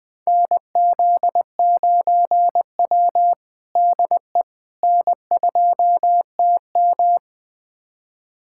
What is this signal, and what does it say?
Morse code, heard as a keyed tone, about 21 wpm, 700 Hz. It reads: NZ9W DE N2TM
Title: USA Callsign DE Callsign - 4 Characters - 20wpm